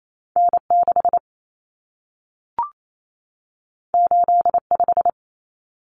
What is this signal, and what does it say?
Morse code, heard as a keyed tone, about 28 wpm, 700 Hz. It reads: N6 85
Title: Common Cut Numbers - 2-Digit 28wpm